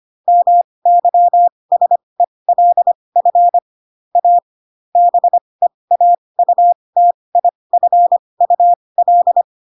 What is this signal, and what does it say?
Morse code, heard as a keyed tone, about 25 wpm, spaced as for 20 wpm, 700 Hz. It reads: MYSELF A BEAUTIFUL
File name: Pinocchio / Pinocchio ch2 15 25 ik1hge